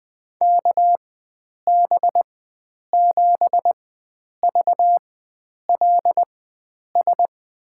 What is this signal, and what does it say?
Morse code, heard as a keyed tone, about 21 wpm, spaced as for 9 wpm, 700 Hz. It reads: KB7VLS
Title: USA Call Signs - 6 Characters - 4x Character Spacing - 20wpm